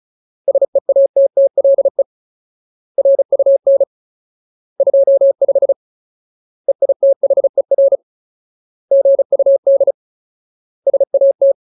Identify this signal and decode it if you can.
Morse code, heard as a keyed tone, about 34 wpm, 550 Hz. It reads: SEATTLE RUN 25 EITHER GUD SAT
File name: Ultimate Word List / 35WPM Ultimate